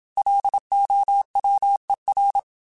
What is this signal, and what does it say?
Morse code, heard as a keyed tone, about 27 wpm, 800 Hz. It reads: LOWER
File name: Technical Subjects / Technical Subjects-mp3s / XC @25